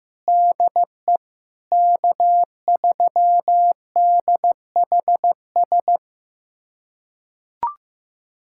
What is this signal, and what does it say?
Morse code, heard as a keyed tone, about 15 wpm, 700 Hz. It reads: DE K3DHS
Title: USA Callsign DE Callsign - 5 Characters - 15wpm